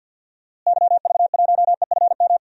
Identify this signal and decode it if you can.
Morse code, heard as a keyed tone, about 46 wpm, 700 Hz. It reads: YV1EFM
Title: DXCC-100 Call Signs - 6 Characters - Repeated 3x - 50wpm